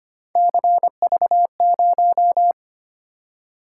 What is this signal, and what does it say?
Morse code, heard as a keyed tone, about 25 wpm, 700 Hz. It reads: CV0
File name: Three Letter-Number Combinations - 25wpm